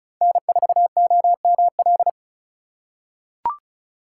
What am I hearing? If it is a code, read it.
Morse code, heard as a keyed tone, about 36 wpm, 700 Hz. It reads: N4OML
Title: USA Call Signs - 5 Characters - 2x Character Spacing - 35wpm